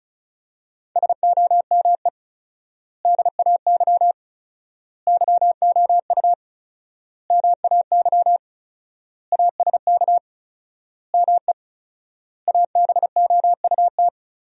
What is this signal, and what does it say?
Morse code, heard as a keyed tone, about 35 wpm, 700 Hz. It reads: SOME DAY YOU MAY ASK ME ABOUT
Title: 4x Word-Spacing - Sentences from Top 200 Words - 35wpm